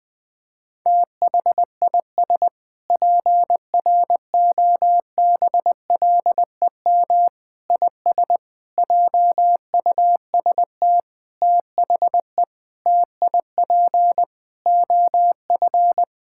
Morse code, heard as a keyed tone, about 20 wpm, 700 Hz. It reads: THIS PROBLEM IS JUST THE TIP OF
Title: Sentences with 7-Word Phrases - 20wpm